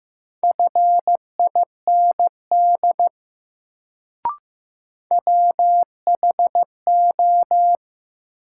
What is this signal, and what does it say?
Morse code, heard as a keyed tone, about 15 wpm, 700 Hz. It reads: FIND WHO
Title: Sentences from Top 100 Words - New Word Drill - 15wpm